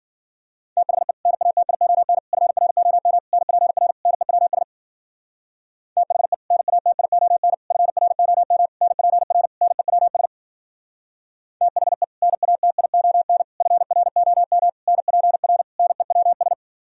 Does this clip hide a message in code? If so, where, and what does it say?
Yes: Morse code from 0.8 to 16.5 s, heard as a keyed tone, about 52 wpm, 700 Hz. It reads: THE NATION FROM NPR NEWS THE NATION FROM NPR NEWS THE NATION FROM NPR NEWS